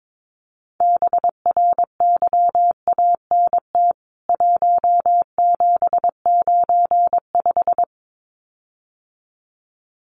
Morse code, heard as a keyed tone, about 22 wpm, 700 Hz. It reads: BRYANT 1795
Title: CWT Exchange 22wpm